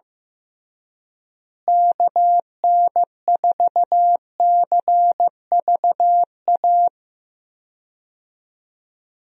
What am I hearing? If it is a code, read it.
Morse code, heard as a keyed tone, about 15 wpm, 700 Hz. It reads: KN4CVA
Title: USA Call Signs - 6 Characters - Repeated 2x - 15wpm